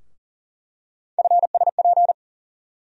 Morse code, heard as a keyed tone, about 40 wpm, 700 Hz. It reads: FSP
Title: USA Call Sign Postfixes - 3 Letters 40wpm